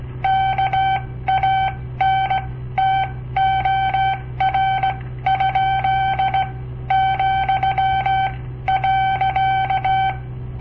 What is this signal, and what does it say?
Morse code, heard as a keyed tone, about 17 wpm, spaced as for 14 wpm, 770 Hz. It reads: KANTOR? ,.
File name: papertapes / local / Code Practice-01